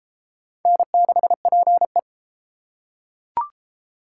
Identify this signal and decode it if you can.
Morse code, heard as a keyed tone, about 32 wpm, 700 Hz. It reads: N6PE